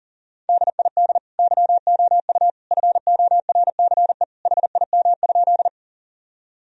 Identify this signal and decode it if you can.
Morse code, heard as a keyed tone, about 37 wpm, 700 Hz. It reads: DID YOU FORCE HIM?
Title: Sentences from Top 1000 Words - Review Part 3 - 40wpm